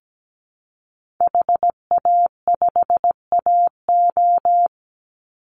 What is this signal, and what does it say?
Morse code, heard as a keyed tone, about 17 wpm, 700 Hz. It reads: HA5AO
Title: Call Signs Repeated 3x - 17wpm